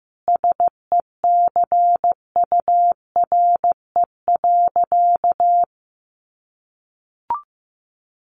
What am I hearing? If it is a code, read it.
Morse code, heard as a keyed tone, about 15 wpm, 700 Hz. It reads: SECURE.
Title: Sentences from Top 1200 Words 15wpm